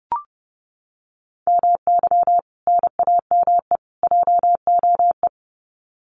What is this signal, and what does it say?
Morse code, heard as a keyed tone, about 30 wpm, 700 Hz. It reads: MY NAME JOE
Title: Common QSO Phrases 30wpm